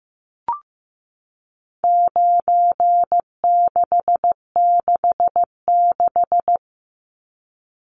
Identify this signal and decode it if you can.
Morse code, heard as a keyed tone, about 15 wpm, 700 Hz. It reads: 9666
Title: Numbers - 4-Digit 15wpm